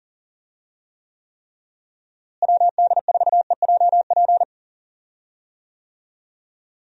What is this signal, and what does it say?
Morse code, heard as a keyed tone, about 37 wpm, 700 Hz. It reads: WD4EJP
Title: USA Call Signs - 6 Characters - Repeated 3x - 40wpm